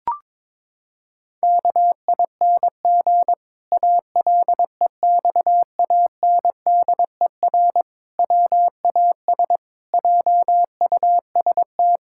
Morse code, heard as a keyed tone, about 21 wpm, 700 Hz. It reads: KING ALEXANDER WAS JUST
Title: Pangram Challenge v2 - 22wpm